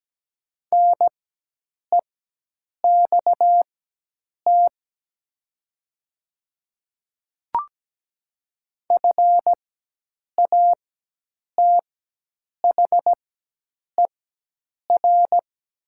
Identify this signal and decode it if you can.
Morse code, heard as a keyed tone, about 17 wpm, spaced as for 8 wpm, 700 Hz. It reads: NEXT FATHER
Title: 4x Character Spacing - Top 300 Words - 17wpm